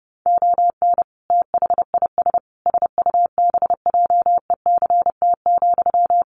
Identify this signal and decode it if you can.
Morse code, heard as a keyed tone, about 28 wpm, 700 Hz. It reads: ON THIS SUBJECT,
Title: Sentences from Top 800 Words 30wpm